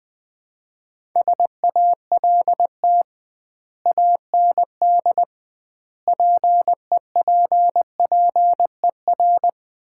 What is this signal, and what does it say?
Morse code, heard as a keyed tone, about 20 wpm, 700 Hz. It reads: SALT AND PEPPER
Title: 2x Word-Spacing - Sets of 5 Words - 20wpm